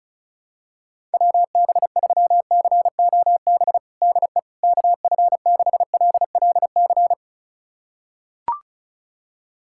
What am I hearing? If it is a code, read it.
Morse code, heard as a keyed tone, about 36 wpm, 690 Hz. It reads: WB3COB DE KF6LLC
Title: USA Callsign DE Callsign - 6 Characters - 35wpm